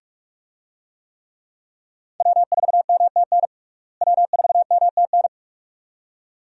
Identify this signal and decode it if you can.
Morse code, heard as a keyed tone, about 46 wpm, 700 Hz. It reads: W4MTN W4MTN
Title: USA Call Signs - 5 Characters - Repeated 2x - 45wpm